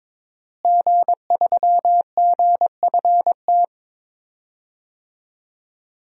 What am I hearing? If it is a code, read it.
Morse code, heard as a keyed tone, about 22 wpm, 700 Hz. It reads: G3GFT
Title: DXCC-100 Call Signs - 5 Characters - Repeated 2x - 22wpm